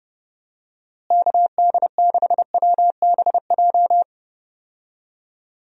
Morse code, heard as a keyed tone, about 30 wpm, 700 Hz. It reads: KD6WBJ
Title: USA Call Signs - 6 Characters - Repeated 3x - 30wpm